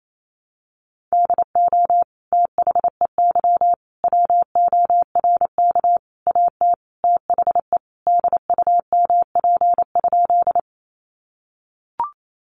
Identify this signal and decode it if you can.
Morse code, heard as a keyed tone, about 28 wpm, 700 Hz. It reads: DO THEY WORK AT THE DUMP?
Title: Sentences from Top 1300 Words 28wpm